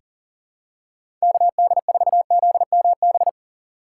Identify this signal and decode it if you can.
Morse code, heard as a keyed tone, about 40 wpm, 700 Hz. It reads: KD4ZMB
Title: USA Call Signs - 6 Characters - 2x Character Spacing - 40wpm